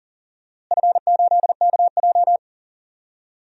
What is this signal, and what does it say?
Morse code, heard as a keyed tone, about 38 wpm, 700 Hz. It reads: F8KJ
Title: DXCC-100 Call Signs - 4 Characters - Repeated 3x - 40wpm